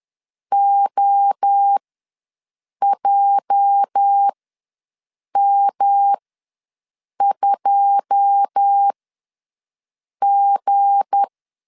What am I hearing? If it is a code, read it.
Morse code, heard as a keyed tone, about 10 wpm, spaced as for 6 wpm, 800 Hz. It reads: OJM2G
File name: LEZIONI / 9 2 ws 10wpm 4130